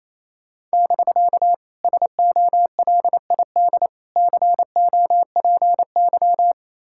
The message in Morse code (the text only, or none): <BK> SOLID COPY